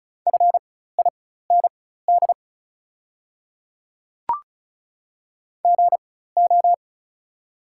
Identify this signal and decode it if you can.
Morse code, heard as a keyed tone, about 36 wpm, spaced as for 16 wpm, 700 Hz. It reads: FIND GO
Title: 4x Character Spacing - Top 100 Words - 35wpm